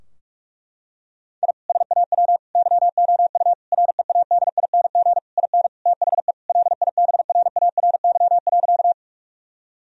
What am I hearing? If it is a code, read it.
Morse code, heard as a keyed tone, about 46 wpm, 700 Hz. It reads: I SAW YOU READING IN THE LIBRARY.
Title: Sentences from Top 1000 Words - Review Part 4 - 45wpm